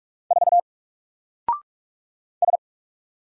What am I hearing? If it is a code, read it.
Morse code, heard as a keyed tone, about 44 wpm, 700 Hz. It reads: 4 S
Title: Single Letter-Number 45wpm